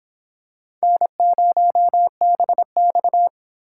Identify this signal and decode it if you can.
Morse code, heard as a keyed tone, about 24 wpm, 700 Hz. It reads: N0BX